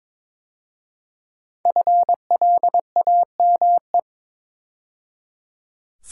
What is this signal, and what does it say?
Morse code, heard as a keyed tone, about 23 wpm, 700 Hz. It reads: FLAME
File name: 5 Letter Words - First 3 Letters Given - 22wpm